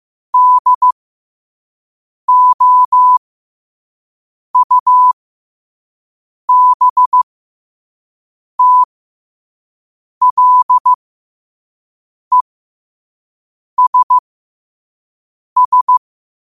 Morse code, heard as a keyed tone, about 15 wpm, spaced as for 5 wpm, 1000 Hz. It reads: DOUBTLESS
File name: qotdfiles / 5wpm